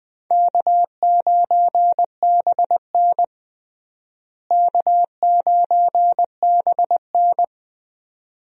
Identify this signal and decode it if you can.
Morse code, heard as a keyed tone, about 20 wpm, 700 Hz. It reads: K9BN K9BN
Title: USA Call Signs - 4 Characters - Repeated 2x - 20wpm